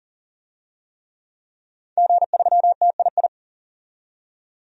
Morse code, heard as a keyed tone, about 40 wpm, 700 Hz. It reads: G3TII
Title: DXCC-100 Call Signs - 5 Characters - 40wpm